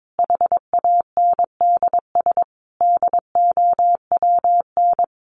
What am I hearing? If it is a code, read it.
Morse code, heard as a keyed tone, about 22 wpm, 700 Hz. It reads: HANDS DOWN
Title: A Sound Of Thunder 22wpm